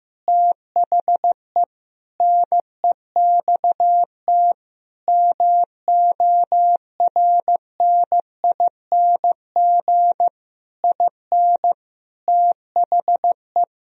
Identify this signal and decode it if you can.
Morse code, heard as a keyed tone, about 15 wpm, 700 Hz. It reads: THE NEXT MORNING IN THE
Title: Sentences from Top 1000 Words - Review Part 3 - 15wpm